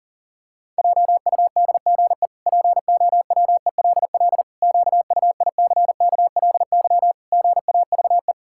Morse code, heard as a keyed tone, about 37 wpm, 700 Hz. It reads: JUDGE POWELL QUICKLY GAVE
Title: Pangram Challenge v2 - 40wpm